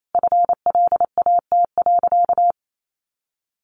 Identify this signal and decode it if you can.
Morse code, heard as a keyed tone, about 28 wpm, 700 Hz. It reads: FLAT.
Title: Sentences from Top 1000 Words 28wpm